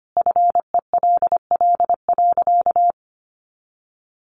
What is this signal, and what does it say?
Morse code, heard as a keyed tone, about 25 wpm, 700 Hz. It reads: FELL.
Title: A Sound Of Thunder 25wpm